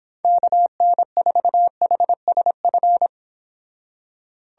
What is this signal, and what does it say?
Morse code, heard as a keyed tone, about 26 wpm, 700 Hz. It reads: KN4HSF